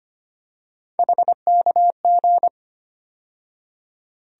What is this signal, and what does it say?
Morse code, heard as a keyed tone, about 25 wpm, 700 Hz. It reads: HKG
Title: USA Call Sign Postfixes - 3 Letters 25wpm